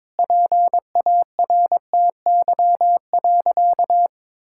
Morse code, heard as a keyed tone, about 22 wpm, 700 Hz. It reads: PARTY.
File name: practice / 22wpm / Sentences with 3-Word Phrases - 22wpm